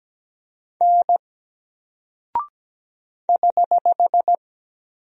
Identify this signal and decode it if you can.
Morse code, heard as a keyed tone, about 17 wpm, 700 Hz. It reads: N <HH>
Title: Single Character (Letter:Number:Punctuation:Prosign) 17wpm